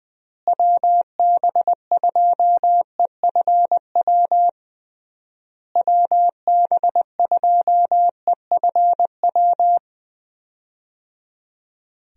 Morse code, heard as a keyed tone, about 20 wpm, 700 Hz. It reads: WB2EFW WB2EFW
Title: USA Call Signs - 6 Characters - Repeated 3x - 20wpm